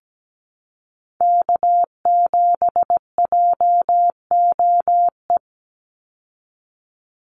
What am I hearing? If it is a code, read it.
Morse code, heard as a keyed tone, about 17 wpm, 700 Hz. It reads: K7JOE
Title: Call Signs - 17wpm